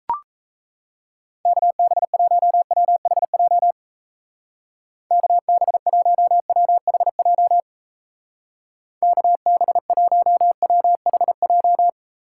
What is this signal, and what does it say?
Morse code, heard as a keyed tone, about 35 wpm, 700 Hz. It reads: KB1WHJ KB1WHJ KB1WHJ
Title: Speed Racing - USA 2x3 Call Signs 28wpm